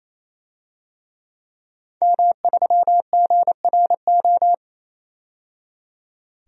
Morse code, heard as a keyed tone, about 28 wpm, 700 Hz. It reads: M3GRO
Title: DXCC-100 Call Signs - 5 Characters - 28wpm